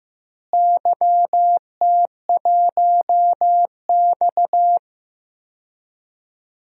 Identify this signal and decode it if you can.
Morse code, heard as a keyed tone, about 15 wpm, 700 Hz. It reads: YT1X
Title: DXCC-100 Call Signs - 4 Characters - Repeated 2x - 15wpm